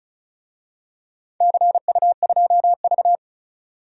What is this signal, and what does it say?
Morse code, heard as a keyed tone, about 36 wpm, 700 Hz. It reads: CU2V
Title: DXCC-100 Call Signs - 4 Characters - 35wpm